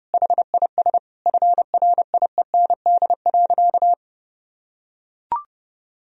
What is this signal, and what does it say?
Morse code, heard as a keyed tone, about 32 wpm, 700 Hz. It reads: HIS FRIEND.
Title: Sentences from Top 1000 Words - Review Part 4 - 30wpm